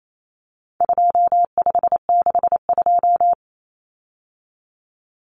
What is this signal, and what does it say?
Morse code, heard as a keyed tone, about 28 wpm, 700 Hz. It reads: 2562
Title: Common Cut Numbers - 4-Digit 28wpm